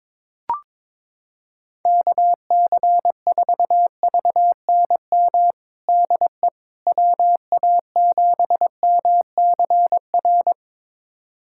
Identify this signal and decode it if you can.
Morse code, heard as a keyed tone, about 22 wpm, 700 Hz. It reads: KC4VNM DE WA7MCR
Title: USA Callsign DE Callsign - 6 Characters - 22wpm